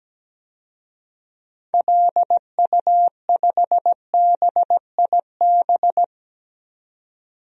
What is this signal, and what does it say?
Morse code, heard as a keyed tone, about 17 wpm, 700 Hz. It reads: LU5BIB
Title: DXCC-100 Call Signs - 6 Characters - 17wpm